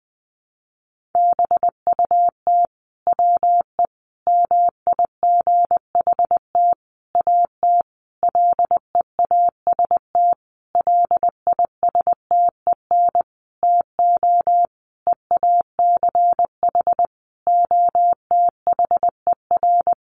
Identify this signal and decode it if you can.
Morse code, heard as a keyed tone, about 20 wpm, 700 Hz. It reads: BUT WE MIGHT AT LEAST LISTEN TO EACH OTHER
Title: Sentences from Top 1300 Words 20wpm